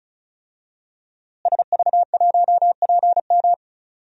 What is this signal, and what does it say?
Morse code, heard as a keyed tone, about 36 wpm, 700 Hz. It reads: SV1PM
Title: DXCC-100 Call Signs - 5 Characters - Repeated 3x - 35wpm